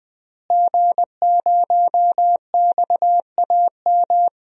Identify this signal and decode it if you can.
Morse code, heard as a keyed tone, about 20 wpm, 700 Hz. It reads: G0XAM
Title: DXCC-100 Call Signs - 5 Characters - Repeated 3x - 20wpm